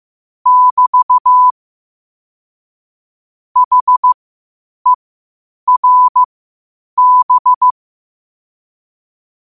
Morse code, heard as a keyed tone, about 15 wpm, spaced as for 8 wpm, 1000 Hz. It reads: <BT> HERB
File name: qotdfiles / 8wpm